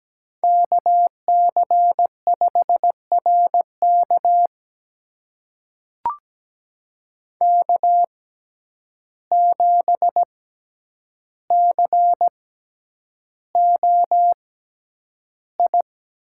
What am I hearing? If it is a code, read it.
Morse code, heard as a keyed tone, about 17 wpm, spaced as for 9 wpm, 700 Hz. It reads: KC5RK K7COI
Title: USA Call Signs - 5 Characters - 6x Character Spacing - 17wpm